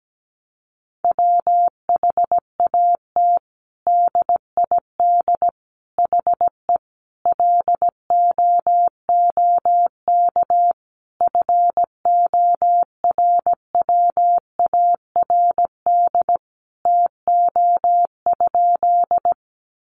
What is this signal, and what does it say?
Morse code, heard as a keyed tone, about 17 wpm, 700 Hz. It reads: WHAT DID HE LOOK FORWARD TO?
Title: Sentences from Top 1300 Words 17wpm